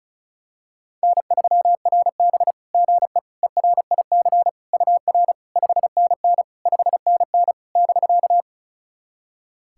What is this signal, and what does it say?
Morse code, heard as a keyed tone, about 36 wpm, 700 Hz. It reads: N3RB GE ERIC UR 5NN 5NN <BK>
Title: SOTA QSO Practice 35wpm